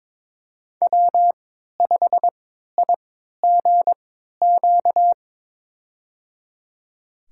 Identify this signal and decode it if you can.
Morse code, heard as a keyed tone, about 22 wpm, spaced as for 12 wpm, 700 Hz. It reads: W5IGQ
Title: USA Call Signs - 3x Character Spacing - 22wpm